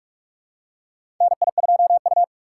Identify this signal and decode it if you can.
Morse code, heard as a keyed tone, about 46 wpm, 700 Hz. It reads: NI2U
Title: USA Call Signs - 2x Character Spacing - 45wpm